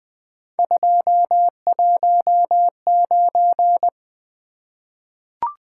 Morse code, heard as a keyed tone, about 20 wpm, 700 Hz. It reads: 219
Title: Three Letter-Number Combinations - 20wpm